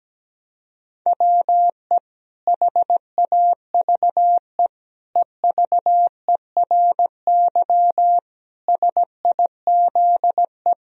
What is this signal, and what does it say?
Morse code, heard as a keyed tone, about 17 wpm, 700 Hz. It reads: WE HAVE EVERY SIZE
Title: Sentences from Top 1000 Words - Review Part 2 - 17wpm